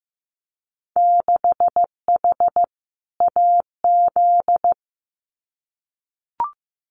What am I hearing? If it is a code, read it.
Morse code, heard as a keyed tone, about 15 wpm, 700 Hz. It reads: 6H AZ
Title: Winter Field Day Exchanges Encore 15wpm